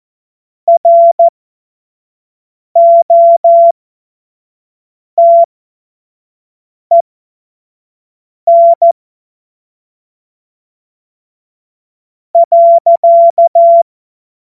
Morse code, heard as a keyed tone, about 14 wpm, spaced as for 4 wpm, 680 Hz. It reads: ROTEN.
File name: Morsekurs / Witze / Witz 21. Februar 2023 25 F70